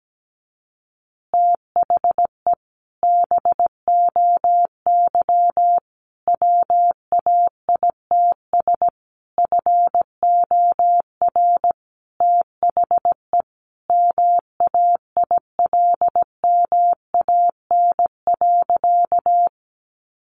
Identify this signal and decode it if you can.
Morse code, heard as a keyed tone, about 17 wpm, 700 Hz. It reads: THE BOY WAITS FOR THE MAILMAN.
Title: Easy Phrases 17wpm